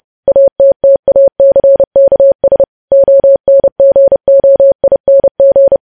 Morse code, heard as a keyed tone, about 30 wpm, 550 Hz. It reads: ATTACKS ONGOING
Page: news headlines in morse code at 30 wpm